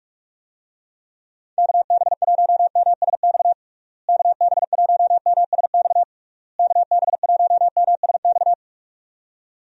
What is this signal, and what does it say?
Morse code, heard as a keyed tone, about 43 wpm, 700 Hz. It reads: KB1MSX KB1MSX KB1MSX
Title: USA Call Signs - 6 Characters - Repeated 3x - 45wpm